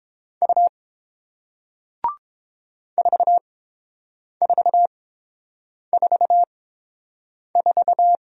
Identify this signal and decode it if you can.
Morse code, heard as a keyed tone, about 30 wpm, spaced as for 7 wpm, 700 Hz. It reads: U 4444